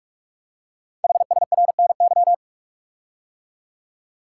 Morse code, heard as a keyed tone, about 44 wpm, 700 Hz. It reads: HSRNY
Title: Code Groups - 5 Letters - 45wpm